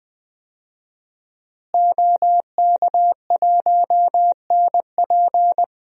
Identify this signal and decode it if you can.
Morse code, heard as a keyed tone, about 20 wpm, 700 Hz. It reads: OK1NP
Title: DXCC-100 Call Signs - 5 Characters - Repeated 3x - 20wpm